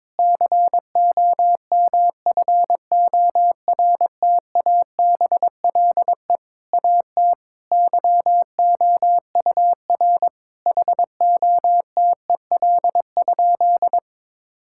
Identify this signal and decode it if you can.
Morse code, heard as a keyed tone, about 22 wpm, 700 Hz. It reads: COMFORTABLE AT YOUR HOTEL?
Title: Sentences from Top 1600 Words 22wpm